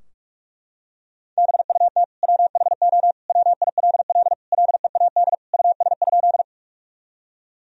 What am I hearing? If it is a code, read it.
Morse code, heard as a keyed tone, about 41 wpm, 700 Hz. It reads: BUT WHO WILL LEAD US?
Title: Sentences from Top 1000 Words - Review Part 2 - 45wpm